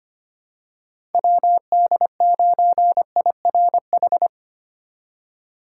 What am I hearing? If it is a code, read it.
Morse code, heard as a keyed tone, about 25 wpm, 700 Hz. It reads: WD9IRH
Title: USA Call Signs - 6 Characters - 3x Character Spacing - 25wpm